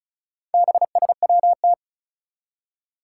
Morse code, heard as a keyed tone, about 36 wpm, 700 Hz. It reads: BSWT
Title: Four Challenging Letter-Number Combinations 35wpm